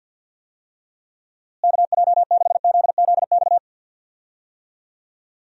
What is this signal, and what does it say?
Morse code, heard as a keyed tone, about 47 wpm, 700 Hz. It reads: KJ6ZZX
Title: USA Call Signs - 6 Characters - Repeated 2x - 50wpm